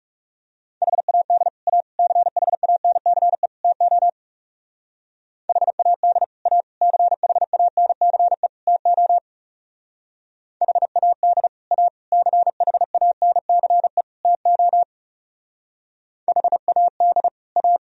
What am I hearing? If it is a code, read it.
Morse code, heard as a keyed tone, about 37 wpm, 700 Hz. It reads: HAD A CHANCE TO HAD A CHANCE TO HAD A CHANCE TO HAD A
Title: Speed-Racing - Sets of 4 Words 30wpm